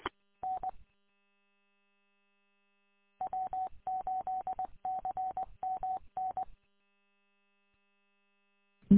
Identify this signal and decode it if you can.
Morse code, heard as a keyed tone, about 20 wpm, 750 Hz. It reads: N W8CMN
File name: w8cmn- -combined